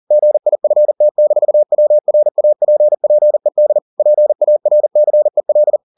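Morse code, heard as a keyed tone, about 40 wpm, 600 Hz. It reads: GIFT-WRAPPED PARCEL
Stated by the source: Morse code at 40 wpm, practice text, CW med Gnister.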